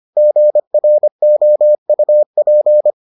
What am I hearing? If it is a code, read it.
Morse code, heard as a keyed tone, about 25 wpm, 600 Hz. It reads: GROUP